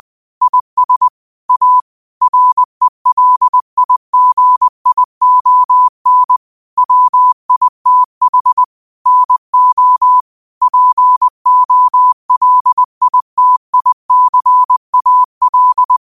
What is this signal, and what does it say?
Morse code, heard as a keyed tone, about 20 wpm, 1000 Hz. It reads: IS A RELIGION WITH NO POLITICAL
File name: qotdfiles / 20wpm